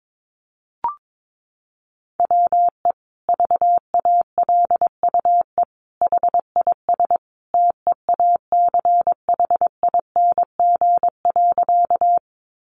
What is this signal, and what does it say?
Morse code, heard as a keyed tone, about 22 wpm, 700 Hz. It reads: WE VALUE HIS TEACHING.
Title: Sentences from Top 1100 Words 22wpm